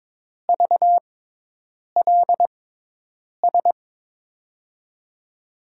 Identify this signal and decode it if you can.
Morse code, heard as a keyed tone, about 22 wpm, spaced as for 8 wpm, 700 Hz. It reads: VLS
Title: USA Call Signs - 6 Characters - 6x Character Spacing - 22wpm